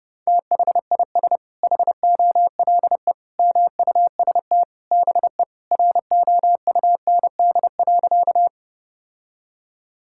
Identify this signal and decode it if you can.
Morse code, heard as a keyed tone, about 30 wpm, 700 Hz. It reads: THIS HOLE MUST BE ROUND.
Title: Sentences from Top 1600 Words 30wpm